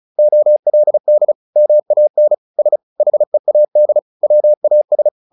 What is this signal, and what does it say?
Morse code, heard as a keyed tone, about 34 wpm, 600 Hz. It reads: OLD MAN S HEAD WAS
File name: the old man and the sea / the old man and the sea 1 35wpm